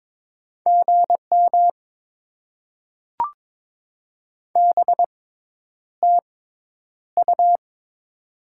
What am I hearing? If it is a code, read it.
Morse code, heard as a keyed tone, about 22 wpm, spaced as for 10 wpm, 690 Hz. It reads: GM BTU
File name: Top 20 QSO Elements - 6x Character Spacing - 22wpm